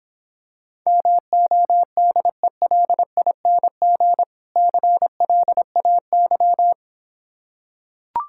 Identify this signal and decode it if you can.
Morse code, heard as a keyed tone, about 26 wpm, 700 Hz. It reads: MODELING CLAY